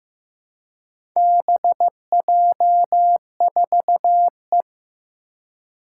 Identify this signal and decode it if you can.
Morse code, heard as a keyed tone, about 15 wpm, 700 Hz. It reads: BJ4E
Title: Four Challenging Letter-Number Combinations 15wpm